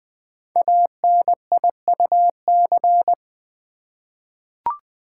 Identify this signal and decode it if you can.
Morse code, heard as a keyed tone, about 20 wpm, 700 Hz. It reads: ANIUC
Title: Code Groups - 5 Letter:Numbers - 20wpm